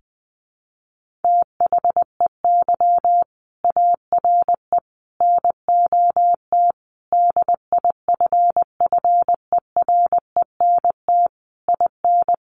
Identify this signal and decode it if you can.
Morse code, heard as a keyed tone, about 20 wpm, 700 Hz. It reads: THEY ARE NOT DIFFERENT IN
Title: Sentences from Top 1400 Words 20wpm